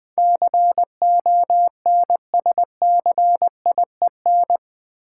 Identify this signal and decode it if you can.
Morse code, heard as a keyed tone, about 20 wpm, 700 Hz. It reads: CONSCIEN
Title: Build A Word - 4 Letters At A Time - 10-Letter Words - 20wpm